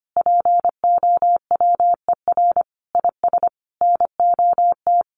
Morse code, heard as a keyed tone, about 25 wpm, 700 Hz. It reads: POWER IS NOT
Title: Sentences from Top 1500 Words 25wpm